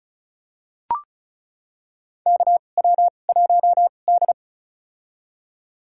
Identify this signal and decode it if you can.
Morse code, heard as a keyed tone, about 35 wpm, spaced as for 25 wpm, 700 Hz. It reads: KW1D
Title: USA Call Signs - 4 Characters - 2x Character Spacing - 35wpm